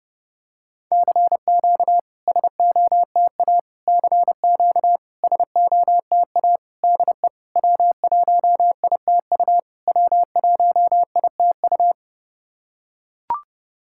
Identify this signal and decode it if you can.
Morse code, heard as a keyed tone, about 30 wpm, 690 Hz. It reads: CQ SOTA CQ SOTA DE W1ITU W1ITU
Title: SOTA QSO Practice 30wpm